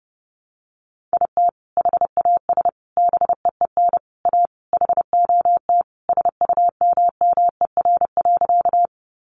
Morse code, heard as a keyed tone, about 30 wpm, 700 Hz. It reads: IT HAS BEEN A HOT SUMMER.
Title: Sentences from Top 1100 Words 30wpm